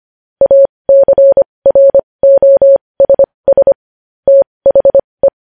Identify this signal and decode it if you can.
Morse code, heard as a keyed tone, about 25 wpm, spaced as for 20 wpm, 550 Hz. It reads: ACROSS THE
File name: podcast / News headlines-1x-25x20